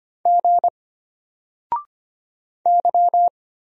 Morse code, heard as a keyed tone, about 25 wpm, 700 Hz. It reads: G Y
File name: Single-Letter-Number-25wpm